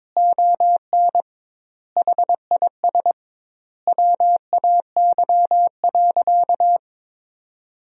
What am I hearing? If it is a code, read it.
Morse code, heard as a keyed tone, about 22 wpm, 700 Hz. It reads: ON HIS WAY.
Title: 2x Word-Spacing - Sentences from Top 300 Words - 22wpm